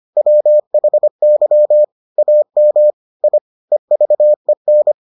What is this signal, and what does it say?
Morse code, heard as a keyed tone, about 25 wpm, 600 Hz. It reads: WHY AM I EVEN